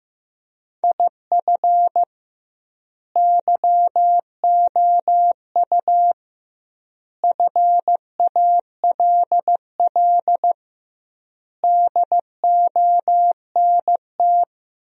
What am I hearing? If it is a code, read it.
Morse code, heard as a keyed tone, about 16 wpm, 700 Hz. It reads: IF YOU FALL DONT
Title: 2x Word-Spacing - Sentences from Top 500 Words - 15wpm